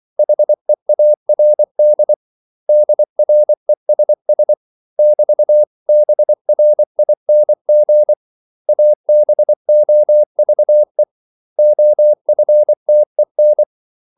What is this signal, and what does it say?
Morse code, heard as a keyed tone, about 24 wpm, 600 Hz. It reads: HEARD DRESS <BT>BRING ABOVE OFTEN
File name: CW / Practice / 500MostCommonEnglishWords24